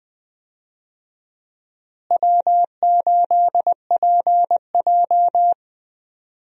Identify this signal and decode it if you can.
Morse code, heard as a keyed tone, about 20 wpm, 700 Hz. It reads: W8PJ
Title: USA Call Signs - 4 Characters - Repeated 3x - 20wpm